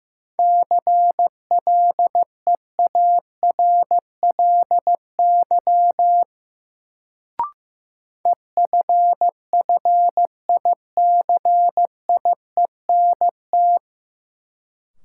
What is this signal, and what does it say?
Morse code, heard as a keyed tone, about 15 wpm, 700 Hz. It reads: CLEARLY EFFICIENT
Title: Sentences from Top 1300 Words - New Word Drill - 15wpm